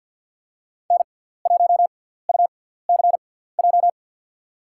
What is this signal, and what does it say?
Morse code, heard as a keyed tone, about 49 wpm, 700 Hz. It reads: N 1 U C J
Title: USA Call Signs - 5 Characters - 6x Character Spacing - 50wpm